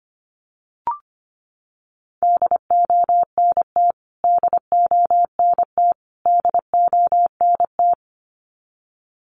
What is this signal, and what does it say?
Morse code, heard as a keyed tone, about 25 wpm, 700 Hz. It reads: DONT DONT DONT
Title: Top 500 Words Repeated 25wpm